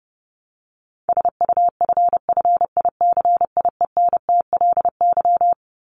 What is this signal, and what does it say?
Morse code, heard as a keyed tone, about 30 wpm, 700 Hz. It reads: SUFFICIENTLY
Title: LY Suffix - Top 100 Words with Base Words 30wpm